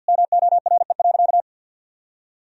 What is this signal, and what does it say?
Morse code, heard as a keyed tone, about 52 wpm, 700 Hz. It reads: MORE.
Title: Sentences from Top 1000 Words - Review Part 1 - 50wpm